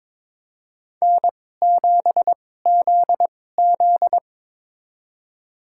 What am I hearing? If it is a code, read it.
Morse code, heard as a keyed tone, about 21 wpm, spaced as for 16 wpm, 700 Hz. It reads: N7ZZ
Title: USA Call Signs - 4 Characters - 2x Character Spacing - 22wpm